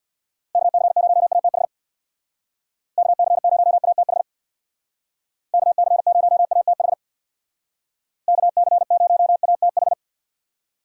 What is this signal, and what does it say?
Morse code, heard as a keyed tone, about 63 wpm, 700 Hz. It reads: KC0ATH KC0ATH KC0ATH KC0ATH